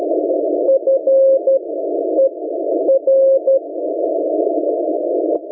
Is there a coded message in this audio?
Yes: Morse code from 0.7 to 3.6 s, heard as a keyed tone, about 12 wpm, spaced as for 9 wpm, 540 Hz. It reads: FER